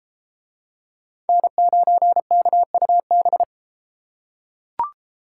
Morse code, heard as a keyed tone, about 32 wpm, 700 Hz. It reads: N9KUB